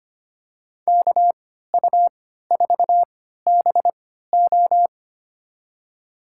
Morse code, heard as a keyed tone, about 25 wpm, spaced as for 14 wpm, 700 Hz. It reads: KU4BO
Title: USA Call Signs - 5 Characters - 3x Character Spacing - 25wpm